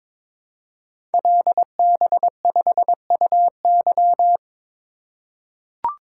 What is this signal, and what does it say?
Morse code, heard as a keyed tone, about 22 wpm, 700 Hz. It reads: LB5UY
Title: Code Groups - 5 Letter:Numbers - 22wpm